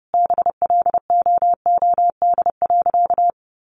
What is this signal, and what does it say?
Morse code, heard as a keyed tone, about 30 wpm, 700 Hz. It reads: BLOOD.
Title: Sentences from Top 1100 Words 30wpm